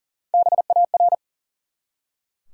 Morse code, heard as a keyed tone, about 40 wpm, 700 Hz. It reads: BAR
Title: Three Letter Words - 40wpm